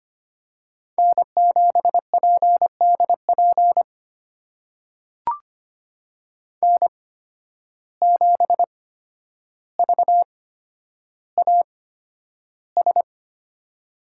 Morse code, heard as a keyed tone, about 25 wpm, spaced as for 11 wpm, 700 Hz. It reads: N7PDP N7VAS
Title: USA Call Signs - 5 Characters - 8x Character Spacing - 25wpm